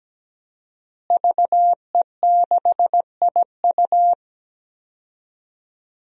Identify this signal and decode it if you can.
Morse code, heard as a keyed tone, about 17 wpm, 700 Hz. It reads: VE6IU
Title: DXCC-100 Call Signs - 5 Characters - Repeated 3x - 17wpm